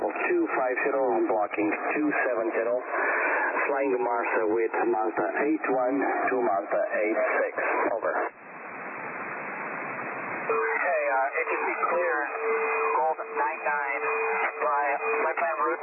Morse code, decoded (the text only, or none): none